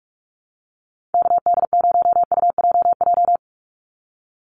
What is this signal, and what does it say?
Morse code, heard as a keyed tone, about 44 wpm, 690 Hz. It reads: KD0UJJ
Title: USA Call Signs 45wpm